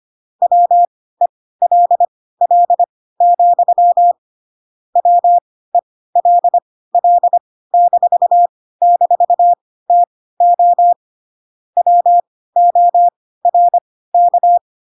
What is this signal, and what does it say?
Morse code, heard as a keyed tone, about 25 wpm, spaced as for 16 wpm, 700 Hz. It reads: WELL, WELL--TO WORK
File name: Pinocchio / Pinocchio ch1 10 25 ik1hge